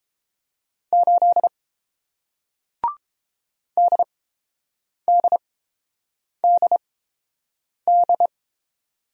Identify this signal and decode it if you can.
Morse code, heard as a keyed tone, about 30 wpm, spaced as for 7 wpm, 700 Hz. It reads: 8 DDDD